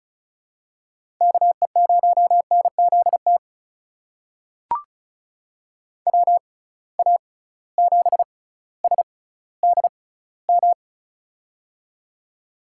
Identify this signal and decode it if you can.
Morse code, heard as a keyed tone, about 36 wpm, spaced as for 18 wpm, 700 Hz. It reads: KE0NZT WA7SDM
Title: USA Call Signs - 6 Characters - 6x Character Spacing - 35wpm